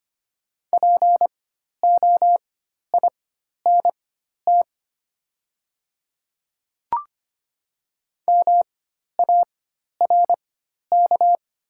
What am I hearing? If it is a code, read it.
Morse code, heard as a keyed tone, about 25 wpm, spaced as for 12 wpm, 700 Hz. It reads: POINT MARK
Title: 4x Character Spacing - Top 500 Words - 25wpm